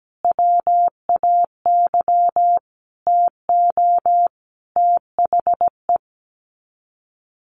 Morse code, heard as a keyed tone, about 17 wpm, 700 Hz. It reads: WAY TO THE
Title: Sets of 5 Words 17wpm